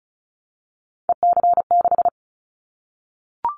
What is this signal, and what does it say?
Morse code, heard as a keyed tone, about 36 wpm, 700 Hz. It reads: EC6
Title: Callsign Prefixes for Top 50 DXCC - 35wpm